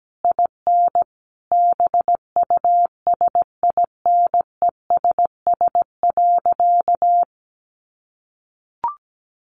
Morse code, heard as a keyed tone, about 17 wpm, 700 Hz. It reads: IN BUSINESS.
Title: Sentences from Top 900 Words 17wpm